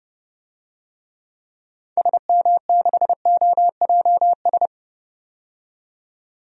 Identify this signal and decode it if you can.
Morse code, heard as a keyed tone, about 30 wpm, 700 Hz. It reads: SM6OJS
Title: DXCC-100 Call Signs - 6 Characters - 30wpm